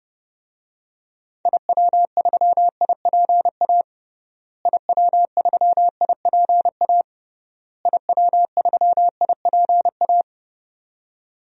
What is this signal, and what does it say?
Morse code, heard as a keyed tone, about 30 wpm, 700 Hz. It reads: IW3IPA IW3IPA IW3IPA
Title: DXCC-100 Call Signs - 6 Characters - Repeated 3x - 30wpm